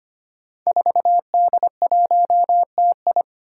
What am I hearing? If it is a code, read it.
Morse code, heard as a keyed tone, about 25 wpm, 700 Hz. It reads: 4D1TI
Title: Five Challenging Letter-Number Combinations 25wpm